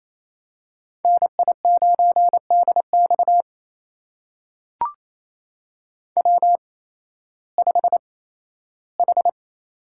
Morse code, heard as a keyed tone, about 28 wpm, spaced as for 15 wpm, 700 Hz. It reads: NI9DX W5H
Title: USA Call Signs - 5 Characters - 8x Character Spacing - 28wpm